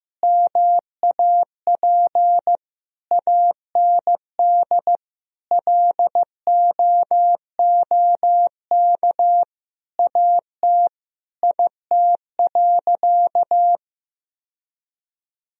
Morse code, heard as a keyed tone, about 15 wpm, 700 Hz. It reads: MAP AND LOOK AT IT.
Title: Sentences from Top 1000 Words - Review Part 2 - 15wpm